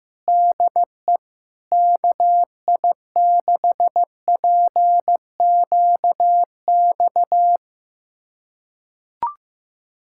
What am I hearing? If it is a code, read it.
Morse code, heard as a keyed tone, about 15 wpm, 700 Hz. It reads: DE KI6PQX
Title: USA Callsign DE Callsign - 6 Characters - 15wpm